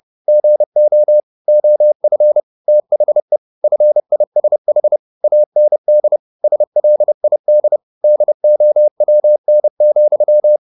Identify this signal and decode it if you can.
Morse code, heard as a keyed tone, about 30 wpm, 600 Hz. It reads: GO OF THE FISH AND SLID DOWN,